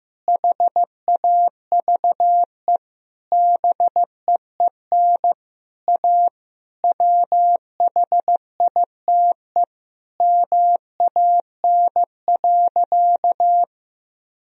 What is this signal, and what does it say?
Morse code, heard as a keyed tone, about 15 wpm, 700 Hz. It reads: HAVE BEEN A WHITE MAN.
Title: Sentences from Top 1000 Words - Review Part 1 - 15wpm